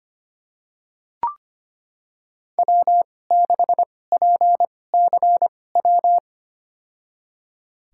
Morse code, heard as a keyed tone, about 25 wpm, spaced as for 18 wpm, 700 Hz. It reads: W6PCW
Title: USA Call Signs - 5 Characters - 2x Character Spacing - 25wpm